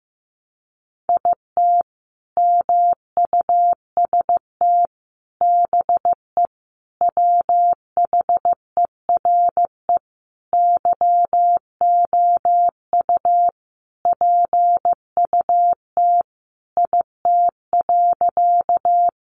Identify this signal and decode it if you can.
Morse code, heard as a keyed tone, about 15 wpm, 700 Hz. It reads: IT MUST BE WHERE YOU PUT IT.
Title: Sentences from Top 200 Words 15wpm